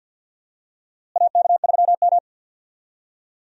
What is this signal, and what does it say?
Morse code, heard as a keyed tone, about 52 wpm, 700 Hz. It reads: AK3M
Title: USA Call Signs - 4 Characters - Repeated 2x - 50wpm